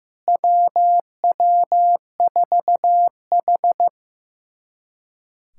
Morse code, heard as a keyed tone, about 16 wpm, 700 Hz. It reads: WW4H
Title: USA Callsign DE Callsign - 4 Characters - 15wpm